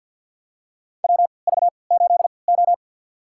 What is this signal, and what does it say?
Morse code, heard as a keyed tone, about 48 wpm, spaced as for 28 wpm, 700 Hz. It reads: WV8O
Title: USA Call Signs - 3x Character Spacing - 50wpm